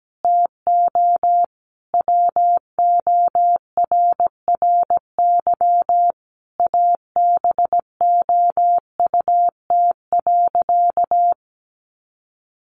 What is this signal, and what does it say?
Morse code, heard as a keyed tone, about 17 wpm, 700 Hz. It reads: TO WORRY ABOUT.
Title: Sentences from Top 1100 Words 17wpm